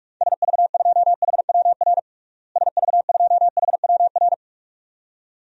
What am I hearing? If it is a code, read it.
Morse code, heard as a keyed tone, about 47 wpm, 700 Hz. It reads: SV2HWR SV2HWR
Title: DXCC-100 Call Signs - 6 Characters - Repeated 2x - 45wpm